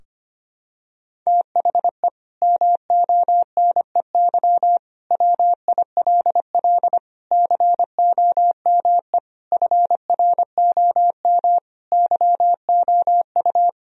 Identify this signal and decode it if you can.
Morse code, heard as a keyed tone, about 25 wpm, 700 Hz. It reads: THE MONEY WILL COME FROM YOU
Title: Sentences from Top 1000 Words - Review Part 4 - 25wpm